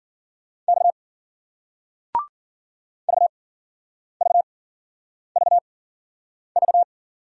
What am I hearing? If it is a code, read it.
Morse code, heard as a keyed tone, about 51 wpm, spaced as for 9 wpm, 700 Hz. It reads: X VVVV